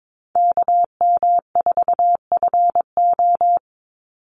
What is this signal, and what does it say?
Morse code, heard as a keyed tone, about 22 wpm, 700 Hz. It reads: KM4FO
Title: Call Signs - 22wpm